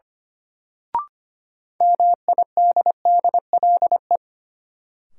Morse code, heard as a keyed tone, about 25 wpm, 700 Hz. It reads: MIDDLE
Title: Sets of 2 Words - New Word Drill - 25wpm